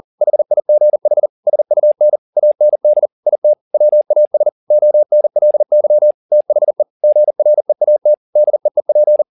Morse code, heard as a keyed tone, about 40 wpm, 600 Hz. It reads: HIGH SUN AND IT WAS ONLY THE GREAT DEEP